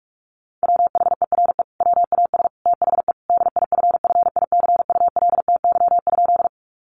Morse code, heard as a keyed tone, about 44 wpm, 700 Hz. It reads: WHERE WAS THE DIFFICULTY?
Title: Sentences from Top 1500 Words 45wpm